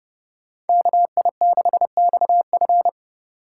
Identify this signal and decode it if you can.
Morse code, heard as a keyed tone, about 30 wpm, 700 Hz. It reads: KI6XF